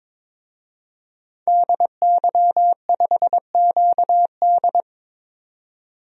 Morse code, heard as a keyed tone, about 22 wpm, 700 Hz. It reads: DY5QD
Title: DXCC-100 Call Signs - 5 Characters - 22wpm